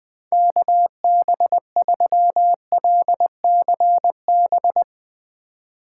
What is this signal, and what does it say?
Morse code, heard as a keyed tone, about 21 wpm, 700 Hz. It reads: KB3LCB
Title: USA Call Signs - 6 Characters - 4x Character Spacing - 20wpm